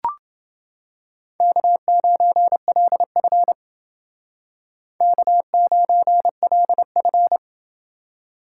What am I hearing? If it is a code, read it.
Morse code, heard as a keyed tone, about 29 wpm, 700 Hz. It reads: K9LF K9LF